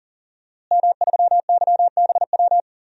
Morse code, heard as a keyed tone, about 41 wpm, 700 Hz. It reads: M3YBW
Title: DXCC-100 Call Signs - 5 Characters - Repeated 3x - 40wpm